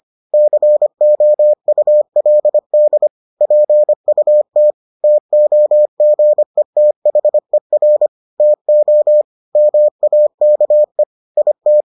Morse code, heard as a keyed tone, about 25 wpm, 600 Hz. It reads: COULD PUT TOGETHER TO MAKE IT